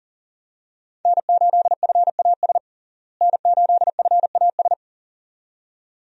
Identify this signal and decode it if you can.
Morse code, heard as a keyed tone, about 40 wpm, 700 Hz. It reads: N8FAS N8FAS
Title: USA Call Signs - 5 Characters - Repeated 2x - 40wpm